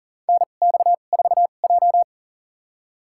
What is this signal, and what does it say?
Morse code, heard as a keyed tone, about 40 wpm, spaced as for 29 wpm, 690 Hz. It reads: NX4J
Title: USA Call Signs - 4 Characters - 2x Character Spacing - 40wpm